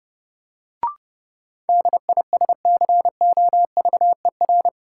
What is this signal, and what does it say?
Morse code, heard as a keyed tone, about 29 wpm, 700 Hz. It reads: DISCOVER
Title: Sentences from Top 1100 Words - New Word Drill Alt - 30wpm